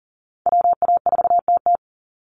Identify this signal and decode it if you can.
Morse code, heard as a keyed tone, about 40 wpm, 700 Hz. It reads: WA4TT
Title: Call Signs Repeated 2x - 40wpm